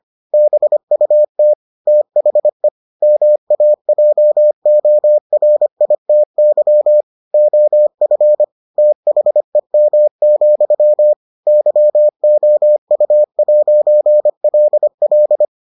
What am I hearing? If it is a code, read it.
Morse code, heard as a keyed tone, about 25 wpm, 600 Hz. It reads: BUT THE MAJORITY OF THEM, YOU'LL